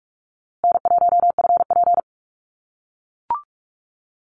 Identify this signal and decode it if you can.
Morse code, heard as a keyed tone, about 44 wpm, 700 Hz. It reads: N1FP
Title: USA Call Signs Repeated 3x 45wpm